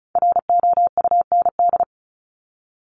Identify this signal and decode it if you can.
Morse code, heard as a keyed tone, about 34 wpm, 700 Hz. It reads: ROUND
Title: Top 200 Words Repeated 35wpm